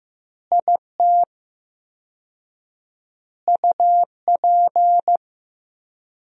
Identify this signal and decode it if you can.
Morse code, heard as a keyed tone, about 15 wpm, 700 Hz. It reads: IT UP
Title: 4x Word-Spacing - Sentences from Top 400 Words - 15wpm